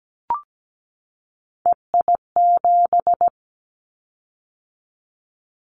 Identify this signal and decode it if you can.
Morse code, heard as a keyed tone, about 17 wpm, 700 Hz. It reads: EI7
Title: Callsign Prefixes for Top 50 DXCC - 17wpm